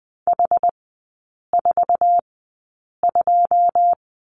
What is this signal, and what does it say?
Morse code, heard as a keyed tone, about 20 wpm, spaced as for 8 wpm, 700 Hz. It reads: H42
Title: 114 Lesson 19e - Letter w - Three Characters 20wpm